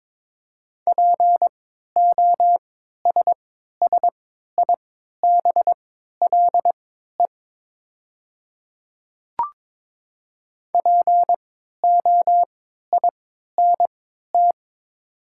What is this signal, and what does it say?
Morse code, heard as a keyed tone, about 22 wpm, spaced as for 12 wpm, 700 Hz. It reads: POSSIBLE POINT
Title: 3x Character Spacing - Top 500 Words - 22wpm